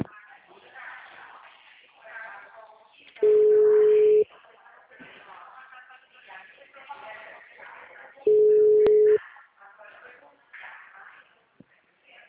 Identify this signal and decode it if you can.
Ringback tone (Europe, 425 Hz), with speech.